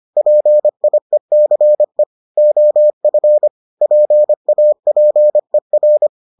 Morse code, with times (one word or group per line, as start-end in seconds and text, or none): 0.2-2.0 PIECE
2.4-3.5 OF
3.8-6.1 PAPER